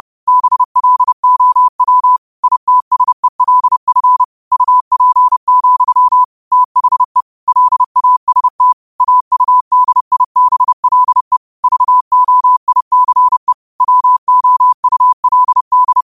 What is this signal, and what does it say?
Morse code, heard as a keyed tone, about 30 wpm, 1000 Hz. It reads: BLOW ITSELF UP, THE LAST AUDIBLE VOICE WOULD
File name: qotdfiles / 30wpm